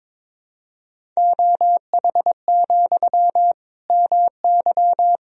Morse code, heard as a keyed tone, about 22 wpm, 700 Hz. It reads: OH, MY
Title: Sentences from Top 1000 Words - Review Part 1 - 22wpm